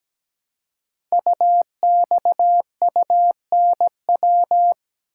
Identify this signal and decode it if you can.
Morse code, heard as a keyed tone, about 17 wpm, 700 Hz. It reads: UXUNW
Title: Code Groups - 5 Letters - 17wpm